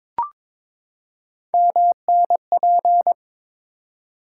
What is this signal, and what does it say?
Morse code, heard as a keyed tone, about 22 wpm, 700 Hz. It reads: MNP
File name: practice / 22wpm / Three Letter-Number Combinations - 22wpm